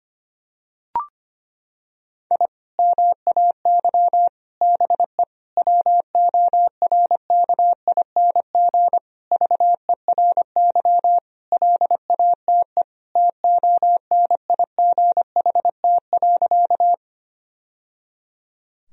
Morse code, heard as a keyed tone, about 25 wpm, 700 Hz. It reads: I MAY BE WORKING VERY LATE TONIGHT.
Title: Sentences from Top 1000 Words - Review Part 4 - 25wpm